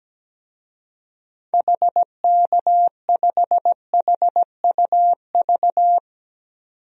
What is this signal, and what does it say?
Morse code, heard as a keyed tone, about 17 wpm, 700 Hz. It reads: HK5HUV
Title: DXCC-100 Call Signs - 6 Characters - 17wpm